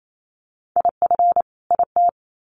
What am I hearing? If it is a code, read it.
Morse code, heard as a keyed tone, about 28 wpm, 700 Hz. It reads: IF IT
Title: Sets of 2 Words 28wpm